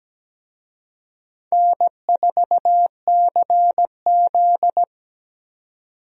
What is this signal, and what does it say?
Morse code, heard as a keyed tone, about 17 wpm, 700 Hz. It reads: N4CZ
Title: USA Call Signs - 4 Characters - 17wpm